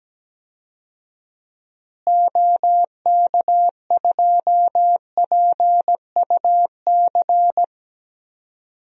Morse code, heard as a keyed tone, about 17 wpm, 700 Hz. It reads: OK2PUC
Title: DXCC-100 Call Signs - 6 Characters - Repeated 3x - 17wpm